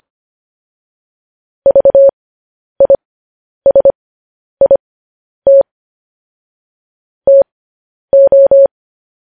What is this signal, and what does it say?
Morse code, heard as a keyed tone, about 25 wpm, spaced as for 10 wpm, 550 Hz. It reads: VISIT TO
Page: news headlines in morse code at 10 wpm